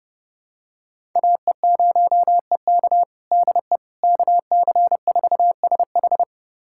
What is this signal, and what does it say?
Morse code, heard as a keyed tone, about 31 wpm, 700 Hz. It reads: AE0EK DE KC4SH
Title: USA Callsign DE Callsign - 5 Characters - 30wpm